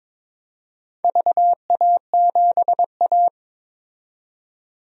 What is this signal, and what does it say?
Morse code, heard as a keyed tone, about 22 wpm, 700 Hz. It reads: VA7A
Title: DXCC-100 Call Signs - 4 Characters - 22wpm